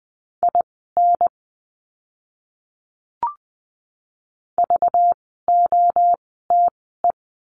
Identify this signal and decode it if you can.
Morse code, heard as a keyed tone, about 20 wpm, spaced as for 14 wpm, 700 Hz. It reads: IN VOTE
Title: 192 Lesson 31f - Letter v - Words 20wpm